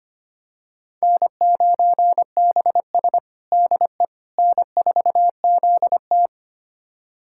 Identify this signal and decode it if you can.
Morse code, heard as a keyed tone, about 25 wpm, 700 Hz. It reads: N9BS DE N4ZT
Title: USA Callsign DE Callsign - 4 Characters - 25wpm